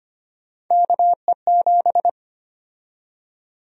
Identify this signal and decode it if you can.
Morse code, heard as a keyed tone, about 25 wpm, 700 Hz. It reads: KE7
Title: USA Call Sign Prefixes - 25wpm